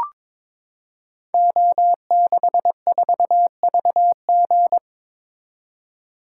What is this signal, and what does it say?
Morse code, heard as a keyed tone, about 22 wpm, 700 Hz. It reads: O64VG
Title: Code Groups - 5 Letter:Numbers - 22wpm